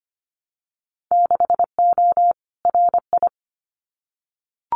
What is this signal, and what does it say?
Morse code, heard as a keyed tone, about 25 wpm, 700 Hz. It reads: 6O RI
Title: Winter Field Day Exchanges Encore 25wpm